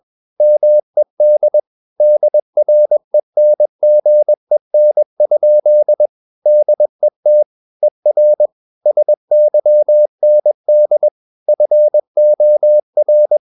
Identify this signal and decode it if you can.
Morse code, heard as a keyed tone, about 21 wpm, 600 Hz. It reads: MED DRENGEN? DET ER SYND FOR
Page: Ferie 20 wpm